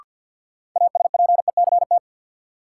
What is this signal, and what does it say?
Morse code, heard as a keyed tone, about 52 wpm, 700 Hz. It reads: ASPECT
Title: Sentences from Top 1300 Words - New Word Drill - 50wpm